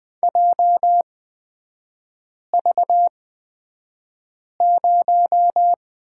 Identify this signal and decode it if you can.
Morse code, heard as a keyed tone, about 20 wpm, spaced as for 5 wpm, 700 Hz. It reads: JV0
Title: 250 Lesson 40e - BK - Three Characters 20wpm